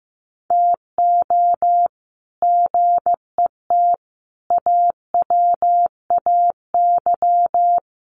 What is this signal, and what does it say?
Morse code, heard as a keyed tone, about 15 wpm, 700 Hz. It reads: TO GET AWAY
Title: Sets of 5 Words Encore2 15wpm